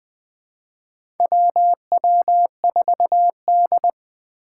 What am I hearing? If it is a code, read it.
Morse code, heard as a keyed tone, about 20 wpm, 700 Hz. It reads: WW4D
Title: 2023 SOTA Activator Call Signs - 20wpm